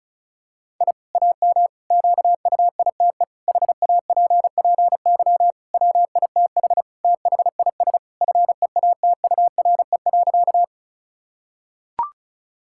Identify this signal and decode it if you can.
Morse code, heard as a keyed tone, about 36 wpm, 700 Hz. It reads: I AM QUITE HAPPY WITH THIS FEATURE.
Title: Sentences from Top 1000 Words - Review Part 5 - 35wpm